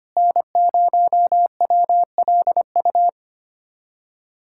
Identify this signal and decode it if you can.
Morse code, heard as a keyed tone, about 25 wpm, 700 Hz. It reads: N0WLU
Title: USA Callsign DE Callsign - 5 Characters - 25wpm